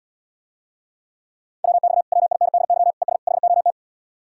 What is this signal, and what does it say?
Morse code, heard as a keyed tone, about 80 wpm, 700 Hz. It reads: WX WINDY ES HOT